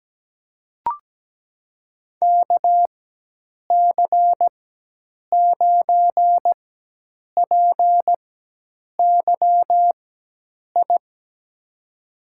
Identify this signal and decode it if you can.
Morse code, heard as a keyed tone, about 17 wpm, spaced as for 8 wpm, 700 Hz. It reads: KC9PYI
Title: USA Call Signs - 6 Characters - 4x Character Spacing - 17wpm